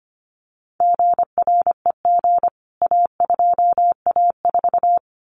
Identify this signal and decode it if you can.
Morse code, heard as a keyed tone, about 25 wpm, 700 Hz. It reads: GREG A2A4
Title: CWT Exchange 25wpm